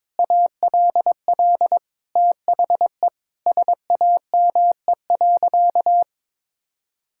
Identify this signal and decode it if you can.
Morse code, heard as a keyed tone, about 22 wpm, 700 Hz. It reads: ALL THE SAME.
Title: Sentences from Top 1600 Words 22wpm